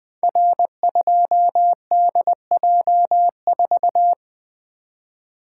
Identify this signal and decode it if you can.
Morse code, heard as a keyed tone, about 20 wpm, 700 Hz. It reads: R2DJ4
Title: Code Groups - 5 Letter:Numbers - 20wpm